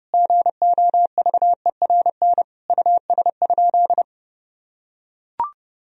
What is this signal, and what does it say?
Morse code, heard as a keyed tone, about 30 wpm, 700 Hz. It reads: GOVERN US?
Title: Sentences from Top 1000 Words - Review Part 2 - 30wpm